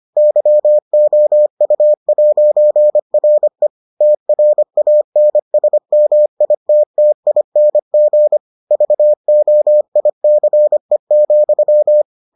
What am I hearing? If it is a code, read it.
Morse code, heard as a keyed tone, about 25 wpm, 600 Hz. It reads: YOU'RE TRANSMITTING VOICE,